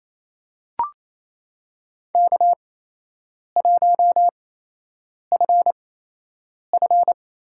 Morse code, heard as a keyed tone, about 29 wpm, spaced as for 8 wpm, 700 Hz. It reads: K1FF
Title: USA Call Signs - 4 Characters - 8x Character Spacing - 28wpm